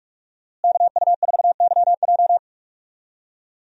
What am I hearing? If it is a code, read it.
Morse code, heard as a keyed tone, about 45 wpm, 700 Hz. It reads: KU4YJ
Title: USA Call Signs - 5 Characters - Repeated 2x - 45wpm